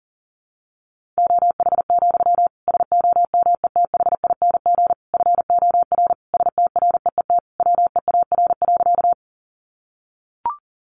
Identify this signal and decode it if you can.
Morse code, heard as a keyed tone, about 40 wpm, 700 Hz. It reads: OH, SOMETHING FOR STREET WEAR.
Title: Sentences from Top 1500 Words 40wpm